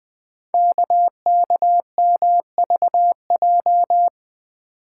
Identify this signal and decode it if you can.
Morse code, heard as a keyed tone, about 20 wpm, 700 Hz. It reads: KKMVJ
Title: Code Groups - 5 Letters - 20wpm